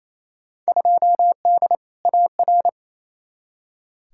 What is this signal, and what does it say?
Morse code, heard as a keyed tone, about 28 wpm, 700 Hz. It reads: 2D AR
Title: Field Day Exchanges Encore - 28wpm